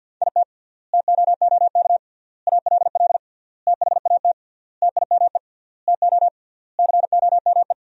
Morse code, heard as a keyed tone, about 52 wpm, 700 Hz. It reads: IT TOOK ALL THAT TIME TO COME
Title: 3x Word-Spacing - Sentences from Top 300 Words - 50wpm